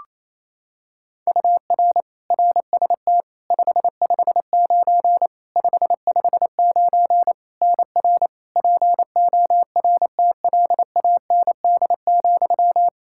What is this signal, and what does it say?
Morse code, heard as a keyed tone, about 28 wpm, 700 Hz. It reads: UR RST 559 559 NR PORTLAND,
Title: Practice QSO - Exchange 1 - Beginning - 28wpm